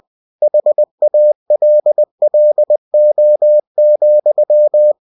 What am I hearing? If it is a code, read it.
Morse code, heard as a keyed tone, about 20 wpm, 600 Hz. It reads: HALLO,
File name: cw-books / winnie-the-pooh / winnie-20wpm-page-0106